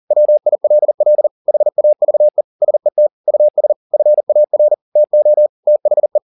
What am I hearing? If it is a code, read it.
Morse code, heard as a keyed tone, about 40 wpm, 600 Hz. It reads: WILL HAVE SET US FAR TO THE